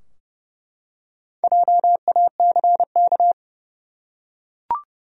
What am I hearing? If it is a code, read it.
Morse code, heard as a keyed tone, about 30 wpm, 700 Hz. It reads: JACK
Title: USA Common Names - Max 5 Letters - 30wpm